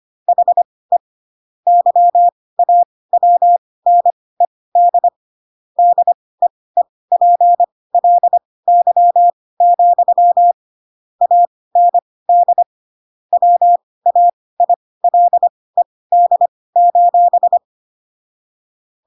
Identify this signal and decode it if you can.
Morse code, heard as a keyed tone, about 25 wpm, spaced as for 18 wpm, 700 Hz. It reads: HE YAWNED DEEPLY, AND WAILED:
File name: Pinocchio / Pinocchio ch7 12 25 ik1hge